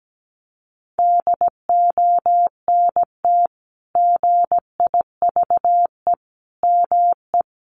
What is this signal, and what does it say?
Morse code, heard as a keyed tone, about 17 wpm, 700 Hz. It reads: DONT GIVE ME
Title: Sentences from Top 600 Words 17wpm